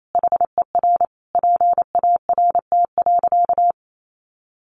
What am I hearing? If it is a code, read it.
Morse code, heard as a keyed tone, about 28 wpm, 700 Hz. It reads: HER PART.
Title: Sentences from Top 1200 Words 28wpm